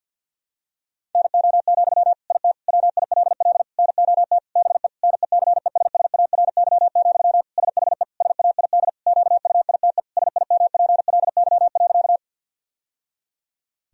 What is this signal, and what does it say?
Morse code, heard as a keyed tone, about 52 wpm, 700 Hz. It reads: NO, IT WILL NOT BE NECESSARY, SHE SAID QUITE SIMPLY.
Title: Sentences from Top 1000 Words - Review Part 4 - 50wpm